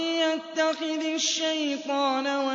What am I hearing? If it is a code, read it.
Speech. There is no code.